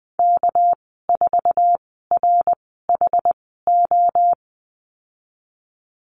Morse code, heard as a keyed tone, about 20 wpm, spaced as for 15 wpm, 700 Hz. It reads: K4RHO
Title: 213 Lesson 34g - Number 8 - Call Signs 20wpm